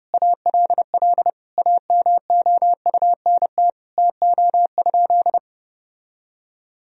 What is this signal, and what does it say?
Morse code, heard as a keyed tone, about 32 wpm, 700 Hz. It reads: ALL AMOUNT TO?
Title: Sentences from Top 1000 Words - Review Part 4 - 30wpm